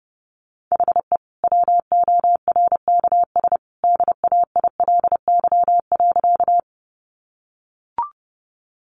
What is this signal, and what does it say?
Morse code, heard as a keyed tone, about 30 wpm, 700 Hz. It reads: HE WORKS DAILY.
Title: Sentences from Top 1100 Words 30wpm